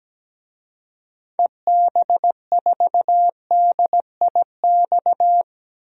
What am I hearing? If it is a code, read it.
Morse code, heard as a keyed tone, about 17 wpm, 700 Hz. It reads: EB4DIX
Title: DXCC-100 Call Signs - 6 Characters - 17wpm